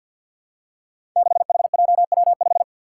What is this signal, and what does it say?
Morse code, heard as a keyed tone, about 53 wpm, 700 Hz. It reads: 6HJW5
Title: Five Challenging Letter-Number Combinations 50wpm